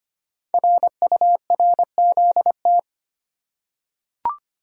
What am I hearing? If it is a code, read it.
Morse code, heard as a keyed tone, about 25 wpm, 700 Hz. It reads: RURZT
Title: Code Groups - 5 Letters - 25wpm